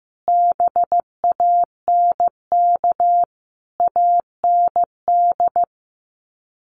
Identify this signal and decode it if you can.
Morse code, heard as a keyed tone, about 15 wpm, 700 Hz. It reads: BANK AND
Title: Sets of 5 Words Encore2 15wpm